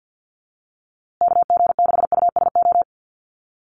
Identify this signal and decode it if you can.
Morse code, heard as a keyed tone, about 48 wpm, 700 Hz. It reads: KG6USO
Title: USA Call Signs Repeated 3x 50wpm